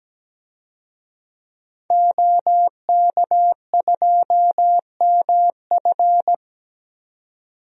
Morse code, heard as a keyed tone, about 17 wpm, 700 Hz. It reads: OK2MF
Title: DXCC-100 Call Signs - 5 Characters - 17wpm